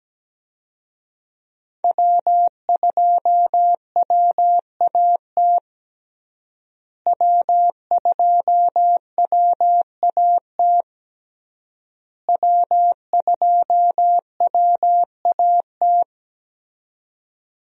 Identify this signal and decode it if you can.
Morse code, heard as a keyed tone, about 17 wpm, 700 Hz. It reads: W2WAT W2WAT W2WAT
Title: USA Call Signs - 5 Characters - Repeated 3x - 17wpm